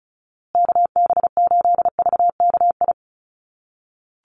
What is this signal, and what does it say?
Morse code, heard as a keyed tone, about 34 wpm, 700 Hz. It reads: KB8VKI
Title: USA Call Signs Repeated 3x 35wpm